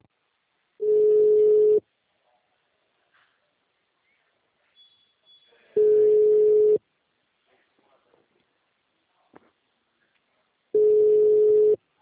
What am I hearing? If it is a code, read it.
Ringback tone (Europe, 425 Hz).